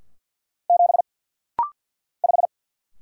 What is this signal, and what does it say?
Morse code, heard as a keyed tone, about 49 wpm, 700 Hz. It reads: 7 5
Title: Single Letter-Number Alt - Rapid-Fire 50wpm